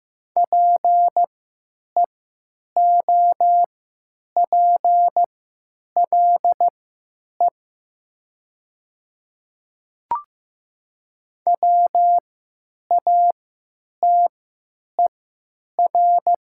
Morse code, heard as a keyed tone, about 15 wpm, spaced as for 9 wpm, 700 Hz. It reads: PEOPLE WATER
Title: 3x Character Spacing - Top 300 Words - 15wpm